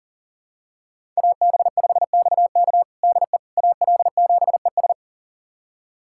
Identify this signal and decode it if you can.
Morse code, heard as a keyed tone, about 37 wpm, 700 Hz. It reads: AB5XK DE AL7ES
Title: USA Callsign DE Callsign - 5 Characters - 40wpm